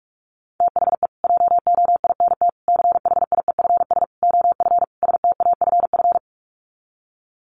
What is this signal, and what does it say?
Morse code, heard as a keyed tone, about 44 wpm, 700 Hz. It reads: THE JOINT CHIEFS OF STAFF
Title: Sets of 5 Words Encore2 45wpm